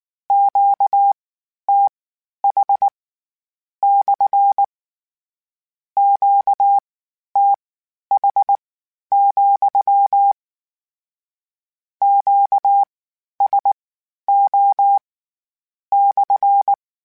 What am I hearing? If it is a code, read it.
Morse code, heard as a keyed tone, about 19 wpm, spaced as for 11 wpm, 800 Hz. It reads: QTH / QTH, QSO /
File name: Q Calls / Q Calls-mp3s / Q Calls SPSL2 @18